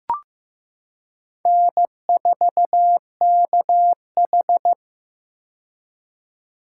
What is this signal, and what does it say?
Morse code, heard as a keyed tone, about 15 wpm, 700 Hz. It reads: N4KH
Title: K1USN SST QSO Practice - 15wpm